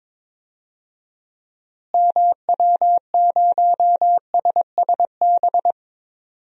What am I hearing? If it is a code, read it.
Morse code, heard as a keyed tone, about 22 wpm, 700 Hz. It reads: MW0SSB
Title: DXCC-100 Call Signs - 6 Characters - Repeated 3x - 22wpm